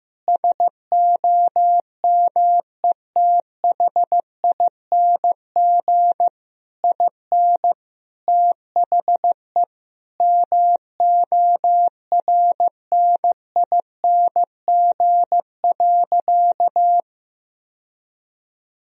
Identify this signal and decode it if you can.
Morse code, heard as a keyed tone, about 15 wpm, 700 Hz. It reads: SOMETHING IN THE MORNING.
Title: Sentences from Top 1000 Words - Review Part 4 - 15wpm